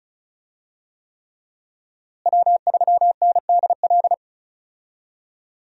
Morse code, heard as a keyed tone, about 36 wpm, 700 Hz. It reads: W3NDL
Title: USA Call Signs - 5 Characters - Repeated 3x - 35wpm